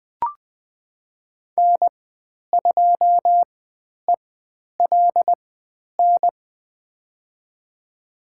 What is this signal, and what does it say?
Morse code, heard as a keyed tone, about 20 wpm, spaced as for 10 wpm, 700 Hz. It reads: N2ELN
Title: 109 Lesson 18g - Letter m - Call Signs 20wpm